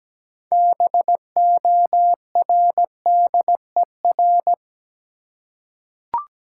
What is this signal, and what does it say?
Morse code, heard as a keyed tone, about 17 wpm, 700 Hz. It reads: BORDER
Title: Six Letter Words - 17wpm